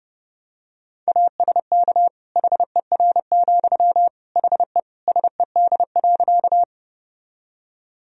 Morse code, heard as a keyed tone, about 30 wpm, 700 Hz. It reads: ASK HER, HE SED.
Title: Sentences from Top 200 Words and Abbreviations 30wpm